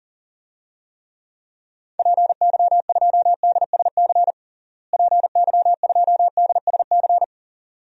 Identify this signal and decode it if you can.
Morse code, heard as a keyed tone, about 37 wpm, 700 Hz. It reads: PY2DSC PY2DSC
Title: DXCC-100 Call Signs - 6 Characters - Repeated 2x - 40wpm